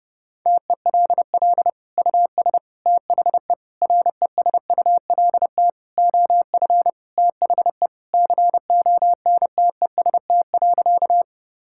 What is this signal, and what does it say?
Morse code, heard as a keyed tone, about 30 wpm, 700 Hz. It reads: TELL US THE RESULT OF THE CONTEST.
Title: Sentences from Top 1600 Words 30wpm